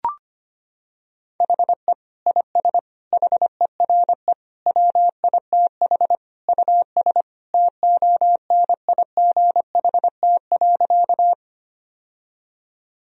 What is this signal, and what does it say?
Morse code, heard as a keyed tone, about 25 wpm, 700 Hz. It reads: HE IS HERE WITH US TONIGHT.
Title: Sentences from Top 1000 Words - Review Part 3 - 25wpm